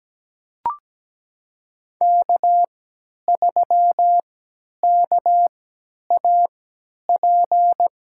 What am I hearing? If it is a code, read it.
Morse code, heard as a keyed tone, about 17 wpm, spaced as for 10 wpm, 700 Hz. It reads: K3KAP
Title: USA Call Signs - 3x Character Spacing - 17wpm